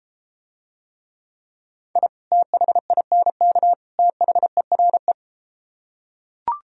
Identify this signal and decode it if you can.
Morse code, heard as a keyed tone, about 34 wpm, 700 Hz. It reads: I THINK THERE